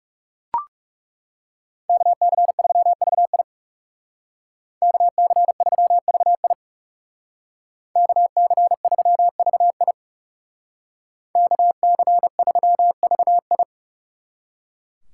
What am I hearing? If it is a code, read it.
Morse code, heard as a keyed tone, about 37 wpm, 700 Hz. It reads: KC3VI KC3VI KC3VI KC3VI
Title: Speed Racing - USA Call Signs 30wpm